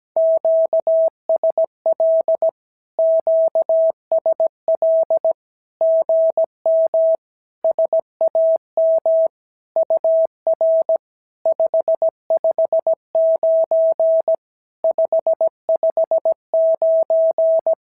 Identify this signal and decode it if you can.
Morse code, heard as a keyed tone, about 17 wpm, 650 Hz. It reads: QSL QSL GM SAM UR 559 559
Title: SOTA QSO Practice 17wpm